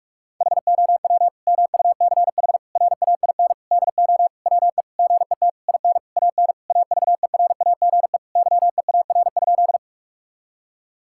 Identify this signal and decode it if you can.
Morse code, heard as a keyed tone, about 41 wpm, 700 Hz. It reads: HOW MUCH RAIN DO WE GET IN AN AVERAGE YEAR?
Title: Sentences from Top 1000 Words - Review Part 6 - 45wpm